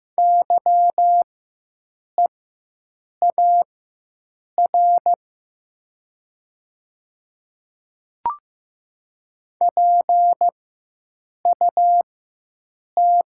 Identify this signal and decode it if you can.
Morse code, heard as a keyed tone, about 15 wpm, spaced as for 7 wpm, 700 Hz. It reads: YEAR PUT
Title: 4x Character Spacing - Top 300 Words - 15wpm